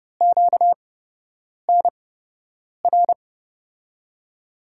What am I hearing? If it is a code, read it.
Morse code, heard as a keyed tone, about 30 wpm, spaced as for 8 wpm, 700 Hz. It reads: QNR
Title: USA Call Signs - 8x Character Spacing - 30wpm